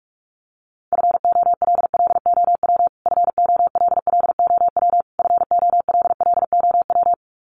Morse code, heard as a keyed tone, about 44 wpm, 700 Hz. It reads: FOLLOW FOLLOW FOLLOW
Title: Top 200 Words Repeated 45wpm